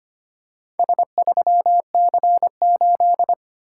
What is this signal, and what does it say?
Morse code, heard as a keyed tone, about 25 wpm, 700 Hz. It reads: S3C8
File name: practice / 25wpm / Four Letter-Number Combinations - 25wpm